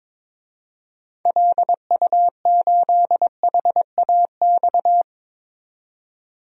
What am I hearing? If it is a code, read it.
Morse code, heard as a keyed tone, about 22 wpm, 700 Hz. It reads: LU8HAX
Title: DXCC-100 Call Signs - 6 Characters - 22wpm